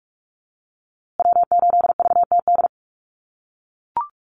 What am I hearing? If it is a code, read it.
Morse code, heard as a keyed tone, about 44 wpm, 700 Hz. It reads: W8VTD
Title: USA Call Signs Repeated 3x 45wpm